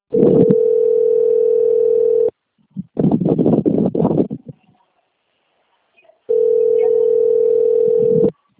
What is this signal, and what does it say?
Ringback tone (North America), 440 + 480 Hz.